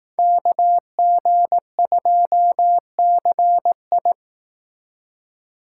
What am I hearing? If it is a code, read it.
Morse code, heard as a keyed tone, about 18 wpm, 700 Hz. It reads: KG2CI
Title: Speed Racing - USA Call Signs 15wpm